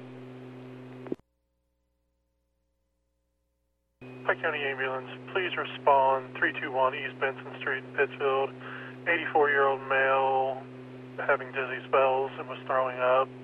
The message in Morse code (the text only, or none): none